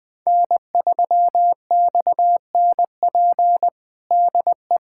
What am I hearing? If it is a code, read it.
Morse code, heard as a keyed tone, about 20 wpm, 700 Hz. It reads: N3XNP DE
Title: USA Callsign DE Callsign - 5 Characters - 20wpm